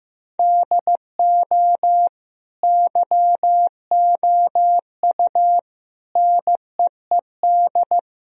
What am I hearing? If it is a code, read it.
Morse code, heard as a keyed tone, about 15 wpm, 700 Hz. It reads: DO YOU NEED
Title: Sentences from Top 1000 Words - Review Part 3 - 15wpm